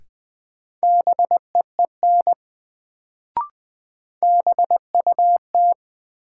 Morse code, heard as a keyed tone, about 20 wpm, 700 Hz. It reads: BEEN BUT
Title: Sets of 2 Words - New Word Drill - 20wpm